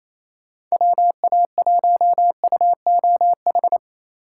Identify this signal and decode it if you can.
Morse code, heard as a keyed tone, about 34 wpm, 700 Hz. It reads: WA1UOH
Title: Speed Racing - USA 2x3 Call Signs 28wpm